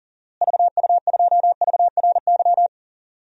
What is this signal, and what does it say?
Morse code, heard as a keyed tone, about 40 wpm, 700 Hz. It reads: VU2VRY
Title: DXCC-100 Call Signs - 6 Characters - Repeated 2x - 40wpm